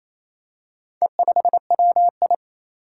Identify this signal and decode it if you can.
Morse code, heard as a keyed tone, about 28 wpm, 700 Hz. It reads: E5WI
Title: Four Challenging Letter-Number Combinations 28wpm